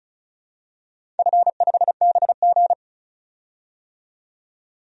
Morse code, heard as a keyed tone, about 35 wpm, 700 Hz. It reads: F5BG
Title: DXCC-100 Call Signs - 4 Characters - 35wpm